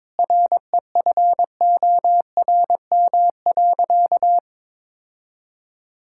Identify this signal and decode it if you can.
Morse code, heard as a keyed tone, about 21 wpm, 700 Hz. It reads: REFORM.
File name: Sentences with 7-Word Phrases - 22wpm